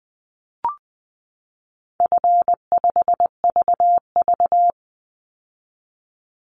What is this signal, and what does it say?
Morse code, heard as a keyed tone, about 20 wpm, 700 Hz. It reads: F5VV
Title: Call Signs - 20wpm